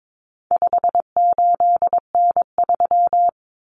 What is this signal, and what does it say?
Morse code, heard as a keyed tone, about 22 wpm, 700 Hz. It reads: 58N3
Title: Common Cut Numbers - 4-Digit 22wpm